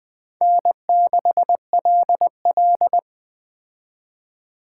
Morse code, heard as a keyed tone, about 19 wpm, 700 Hz. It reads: N6LL